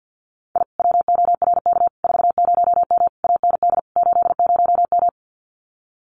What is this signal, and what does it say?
Morse code, heard as a keyed tone, about 50 wpm, 700 Hz. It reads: I WORK 40M AND 80M
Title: Common QSO Phrases Encore 50wpm